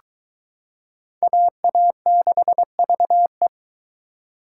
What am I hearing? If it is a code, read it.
Morse code, heard as a keyed tone, about 22 wpm, 700 Hz. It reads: AA6VE